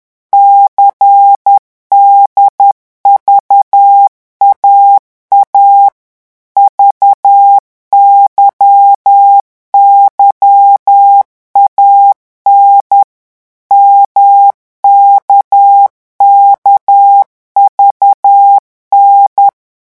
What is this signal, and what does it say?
Morse code, heard as a keyed tone, about 10 wpm, 800 Hz. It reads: CDVAA VYYAN MKKVN
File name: morse mp3 / revision3-vitesse 10 mots